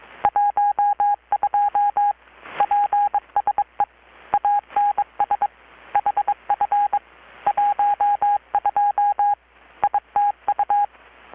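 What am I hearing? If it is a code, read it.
Morse code, heard as a keyed tone, about 23 wpm, 820 Hz. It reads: 12 PSE ANS HF 12 ITU
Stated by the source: Morse from KPH's Night of Nights XX maritime radiotelegraphy event, received in Chiapas, Mexico about 3,800 km away on a Kenwood R-600 with a grounded 10 m long-wire antenna.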